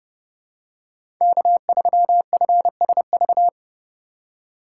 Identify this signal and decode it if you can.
Morse code, heard as a keyed tone, about 30 wpm, 700 Hz. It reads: K3FSV
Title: USA Call Signs - 5 Characters - 2x Character Spacing - 30wpm